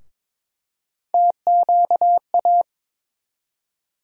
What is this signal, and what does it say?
Morse code, heard as a keyed tone, about 22 wpm, 700 Hz. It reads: TQA
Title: USA Call Sign Postfixes - 3 Letters 22wpm